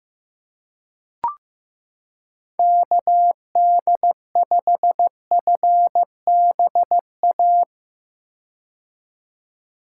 Morse code, heard as a keyed tone, about 15 wpm, 700 Hz. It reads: KD5FBA
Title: K1USN SST QSO Practice - 15wpm